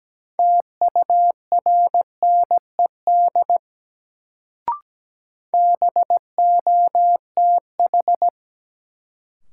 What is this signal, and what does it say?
Morse code, heard as a keyed tone, about 17 wpm, 700 Hz. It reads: TURNED BOTH
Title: Sets of 4 Words - New Word Drill - 17wpm